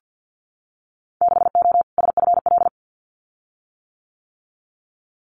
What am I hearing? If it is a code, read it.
Morse code, heard as a keyed tone, about 50 wpm, 700 Hz. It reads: 6O SFL
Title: Winter Field Day Exchanges Encore 50wpm